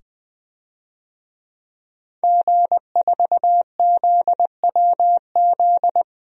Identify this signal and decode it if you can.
Morse code, heard as a keyed tone, about 20 wpm, 700 Hz. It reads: G4ZWZ
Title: DXCC-100 Call Signs - 5 Characters - 20wpm